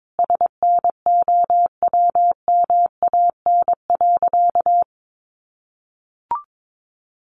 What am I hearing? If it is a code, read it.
Morse code, heard as a keyed tone, about 22 wpm, 700 Hz. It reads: SNOWMAN.
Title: Easy Phrases 22wpm